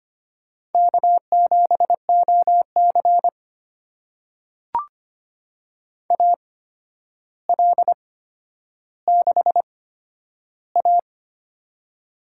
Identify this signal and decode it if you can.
Morse code, heard as a keyed tone, about 25 wpm, spaced as for 11 wpm, 700 Hz. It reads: K7OC AL6A
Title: USA Call Signs - 4 Characters - 8x Character Spacing - 25wpm